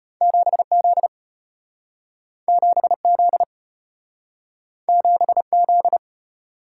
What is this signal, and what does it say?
Morse code, heard as a keyed tone, about 32 wpm, 700 Hz. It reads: 7Z 7Z 7Z